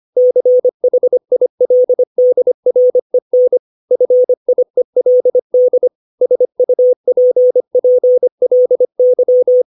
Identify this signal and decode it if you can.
Morse code, heard as a keyed tone, about 25 wpm, 500 Hz. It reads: CHILDREN FIELD SUPPLY
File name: 632 words / 632 Words 25WPM